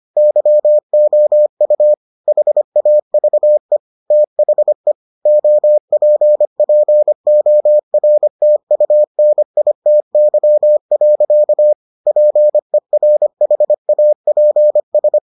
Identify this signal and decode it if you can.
Morse code, heard as a keyed tone, about 25 wpm, 600 Hz. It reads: YOU HAVE THE OPPORTUNITY. PERHAPS